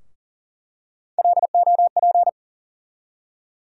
Morse code, heard as a keyed tone, about 40 wpm, 700 Hz. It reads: LOP
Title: USA Call Sign Postfixes - 3 Letters 40wpm